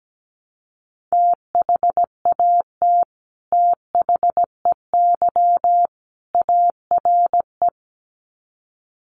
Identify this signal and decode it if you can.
Morse code, heard as a keyed tone, about 17 wpm, 700 Hz. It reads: THAT THEY ARE
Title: Sets of 3 Words 17wpm